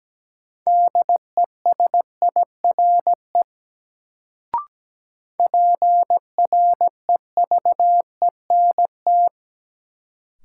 Morse code, heard as a keyed tone, about 17 wpm, 700 Hz. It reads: DESIRE PREVENT
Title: Sentences from Top 1100 Words - New Word Drill - 17wpm